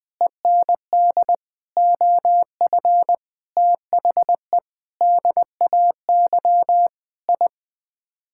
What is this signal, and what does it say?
Morse code, heard as a keyed tone, about 20 wpm, 700 Hz. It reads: END OF THE DAY I
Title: Sets of 8 Words - 20wpm